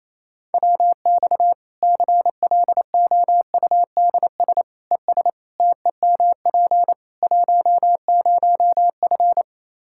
Morse code, heard as a keyed tone, about 28 wpm, 700 Hz. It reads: WX CLOUDS ES TEMP 10F
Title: Practice QSO - Exchange 2 - Info - 28wpm